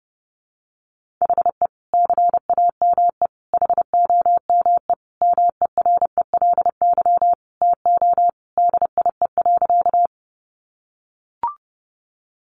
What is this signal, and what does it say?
Morse code, heard as a keyed tone, about 30 wpm, 700 Hz. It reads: HE CAME HOME MERELY TO DIE.
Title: Sentences from Top 1400 Words 30wpm